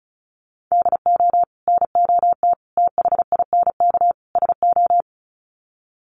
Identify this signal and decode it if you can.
Morse code, heard as a keyed tone, about 34 wpm, 700 Hz. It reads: DO NOT THINK SO
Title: Sets of 4 Words 35wpm